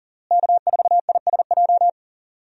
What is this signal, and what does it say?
Morse code, heard as a keyed tone, about 40 wpm, 700 Hz. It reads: K4ISJ
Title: USA Call Signs - 5 Characters - Repeated 2x - 40wpm